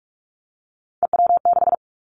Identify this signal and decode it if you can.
Morse code, heard as a keyed tone, about 46 wpm, 700 Hz. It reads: EW6
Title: Callsign Prefixes for Top 50 DXCC - 45wpm